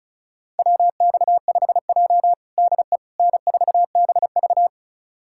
Morse code, heard as a keyed tone, about 36 wpm, 700 Hz. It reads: WX5J DE N4BV
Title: USA Callsign DE Callsign - 4 Characters - 35wpm